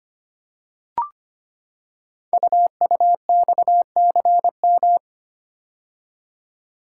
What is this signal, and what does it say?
Morse code, heard as a keyed tone, about 25 wpm, 700 Hz. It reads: UUXCM
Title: Code Groups - 5 Letters - 25wpm